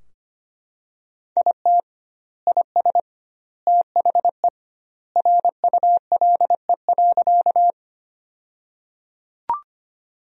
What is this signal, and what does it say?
Morse code, heard as a keyed tone, about 25 wpm, 700 Hz. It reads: IT IS THE RULE.
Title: 2x Word-Spacing - Sentences from Top 400 Words - 25wpm